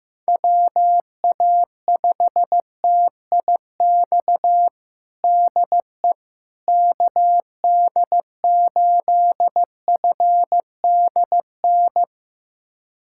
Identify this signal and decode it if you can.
Morse code, heard as a keyed tone, about 15 wpm, 700 Hz. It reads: WA5TIX DE KD8FDN
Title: USA Callsign DE Callsign - 6 Characters - 15wpm